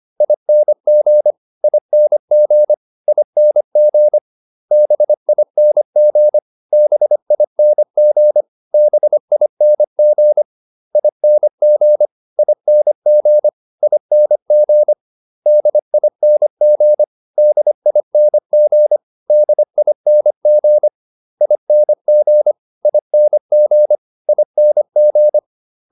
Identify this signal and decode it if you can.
Morse code, heard as a keyed tone, about 25 wpm, 600 Hz. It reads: ING ING ING BING BING BING ING ING ING DING DING DING ING ING ING